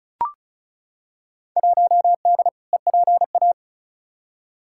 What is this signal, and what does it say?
Morse code, heard as a keyed tone, about 34 wpm, 700 Hz. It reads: 1D EPA
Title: Field Day Exchanges - 35wpm